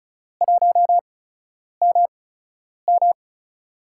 Morse code, heard as a keyed tone, about 34 wpm, spaced as for 9 wpm, 700 Hz. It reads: 1MM
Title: USA Call Signs - 4 Characters - 8x Character Spacing - 35wpm